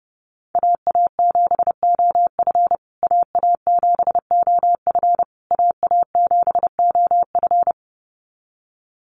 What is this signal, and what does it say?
Morse code, heard as a keyed tone, about 28 wpm, 700 Hz. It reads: AA7OF AA7OF AA7OF
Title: USA Call Signs Repeated 3x 30wpm